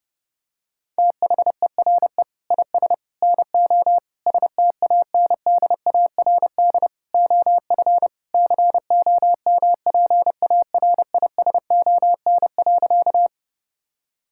Morse code, heard as a keyed tone, about 30 wpm, 700 Hz. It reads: THERE IS NO STANDARD OF COMPARISON.
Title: Sentences from Top 1600 Words 30wpm